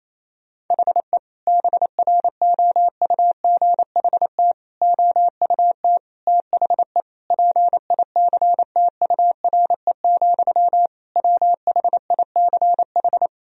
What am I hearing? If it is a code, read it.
Morse code, heard as a keyed tone, about 27 wpm, 700 Hz. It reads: HE BROUGHT OUT THE PICTURE, WHICH
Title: Sentences from Top 1000 Words - Review Part 3 - 28wpm